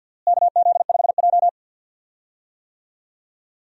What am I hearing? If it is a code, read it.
Morse code, heard as a keyed tone, about 52 wpm, 690 Hz. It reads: KZ5J
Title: USA Call Signs - 4 Characters - Repeated 2x - 50wpm